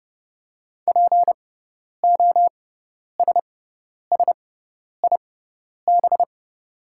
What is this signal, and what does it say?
Morse code, heard as a keyed tone, about 30 wpm, spaced as for 10 wpm, 700 Hz. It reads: POSSIB
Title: 6x Character Spacing - Top 500 Words - 30wpm